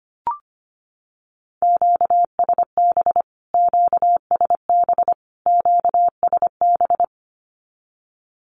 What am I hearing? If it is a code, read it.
Morse code, heard as a keyed tone, about 25 wpm, 700 Hz. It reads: QSB QSB QSB
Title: Q-Codes Repeated 3x 25wpm